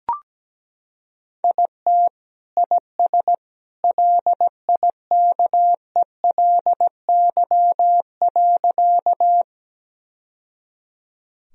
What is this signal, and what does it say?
Morse code, heard as a keyed tone, about 17 wpm, 700 Hz. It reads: IT IS LIKELY.
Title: Sentences from Top 1000 Words - Review Part 4 - 17wpm